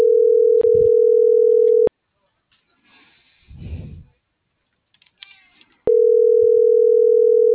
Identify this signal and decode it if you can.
Ringback tone (North America), 440 + 480 Hz.